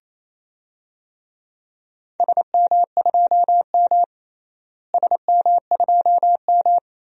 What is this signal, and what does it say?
Morse code, heard as a keyed tone, about 27 wpm, 700 Hz. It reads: SM2M SM2M
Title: DXCC-100 Call Signs - 4 Characters - Repeated 2x - 28wpm